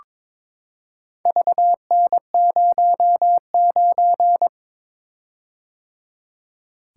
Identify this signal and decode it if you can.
Morse code, heard as a keyed tone, about 22 wpm, 700 Hz. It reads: VN09
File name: practice / 22wpm / Four Letter-Number Combinations - 22wpm